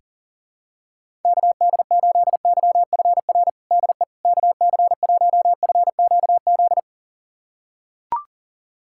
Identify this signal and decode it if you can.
Morse code, heard as a keyed tone, about 40 wpm, 700 Hz. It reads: KD8YFR DE KC1FQZ
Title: USA Callsign DE Callsign 40wpm